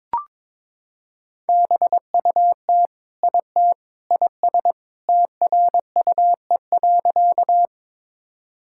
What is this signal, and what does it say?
Morse code, heard as a keyed tone, about 22 wpm, 700 Hz. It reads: BUT IT IS TRUE.
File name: practice / 22wpm / Sentences with 3-Word Phrases - 22wpm